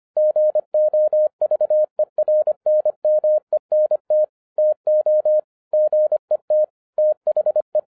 Morse code, heard as a keyed tone, about 25 wpm, 610 Hz. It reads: GOVERNMENT TO GET THE
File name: 2012-01-mp3 / n08-ITS MY FAULT, BUT-25wpm